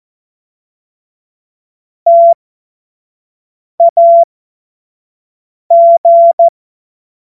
Morse code, heard as a keyed tone, about 14 wpm, spaced as for 5 wpm, 680 Hz. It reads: TAG
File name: Morsekurs / Witze / Witz 15. November 2022 25 F70